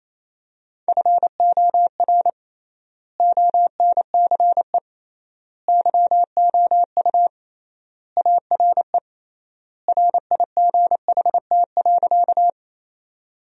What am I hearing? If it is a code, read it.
Morse code, heard as a keyed tone, about 28 wpm, 700 Hz. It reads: FOR ONCE YOU ARE RIGHT.
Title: 3x Word-Spacing - Sentences from Top 300 Words - 28wpm